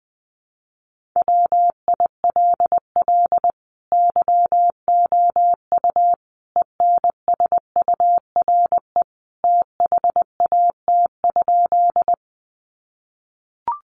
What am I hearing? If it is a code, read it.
Morse code, heard as a keyed tone, about 20 wpm, 700 Hz. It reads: WILL YOU ENSURE THAT?
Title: Sentences from Top 1100 Words 20wpm